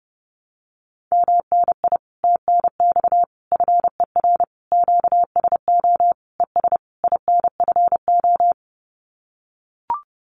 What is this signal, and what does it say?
Morse code, heard as a keyed tone, about 30 wpm, 700 Hz. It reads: MNI TNX FER QSO ES INFO
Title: Common QSO Phrases Encore 30wpm